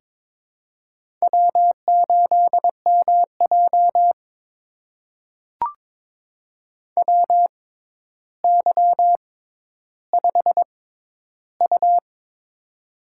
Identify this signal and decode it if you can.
Morse code, heard as a keyed tone, about 22 wpm, spaced as for 11 wpm, 700 Hz. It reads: W8MJ WY5U
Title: USA Call Signs - 4 Characters - 6x Character Spacing - 22wpm